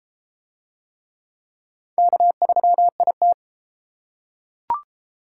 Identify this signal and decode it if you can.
Morse code, heard as a keyed tone, about 32 wpm, 700 Hz. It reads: K3IT